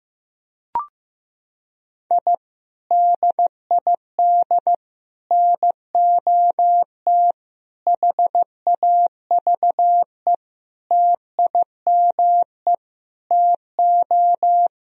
Morse code, heard as a keyed tone, about 15 wpm, 700 Hz. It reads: I DID NOT HAVE TIME TO
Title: Sentences from Top 1000 Words - Review Part 3 - 15wpm